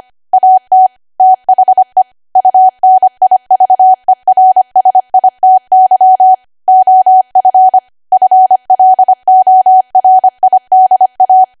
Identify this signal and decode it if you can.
Morse code, heard as a keyed tone, about 25 wpm, 750 Hz. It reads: AT THE UNIVERSITY OF FLORIDA
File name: Archive / 25 WPM / 25WPM